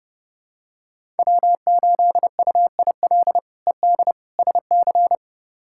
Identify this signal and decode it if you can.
Morse code, heard as a keyed tone, about 32 wpm, 700 Hz. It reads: W8UIL ED SC
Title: K1USN SST QSO Practice - 30wpm